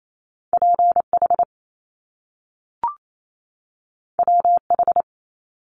Morse code, heard as a keyed tone, about 28 wpm, 700 Hz. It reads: PH WH
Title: Top Two Letter Combinations in Words 28wpm